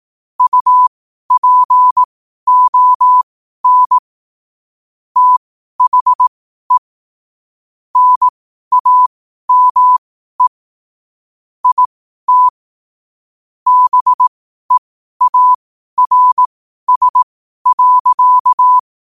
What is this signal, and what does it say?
Morse code, heard as a keyed tone, about 18 wpm, spaced as for 12 wpm, 1000 Hz. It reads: UPON THE NAME IT BEARS.